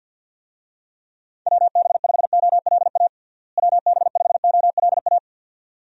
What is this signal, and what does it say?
Morse code, heard as a keyed tone, about 52 wpm, 700 Hz. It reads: WB5OLA WB5OLA
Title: USA Call Signs - 6 Characters - Repeated 2x - 50wpm